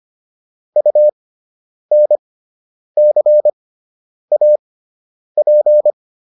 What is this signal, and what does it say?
Morse code, heard as a keyed tone, about 25 wpm, spaced as for 9 wpm, 600 Hz. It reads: UNCAP